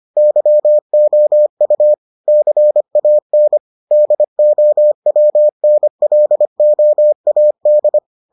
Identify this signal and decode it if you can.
Morse code, heard as a keyed tone, about 25 wpm, 600 Hz. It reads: YOU CAN DOWNLOAD